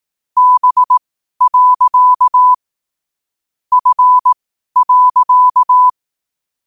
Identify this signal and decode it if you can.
Morse code, heard as a keyed tone, about 18 wpm, spaced as for 12 wpm, 1000 Hz. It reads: B. F.
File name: qotdfiles / 12wpm